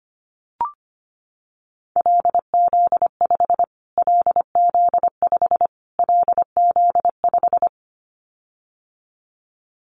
Morse code, heard as a keyed tone, about 25 wpm, 700 Hz. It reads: LZ5 LZ5 LZ5
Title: Callsign Prefixes for Top 25 DXCC - Repeated 3x - 25wpm